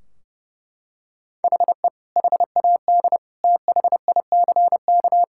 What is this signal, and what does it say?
Morse code, heard as a keyed tone, about 28 wpm, 700 Hz. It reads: HE HAD THICK
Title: Sentences from Top 1600 Words 30wpm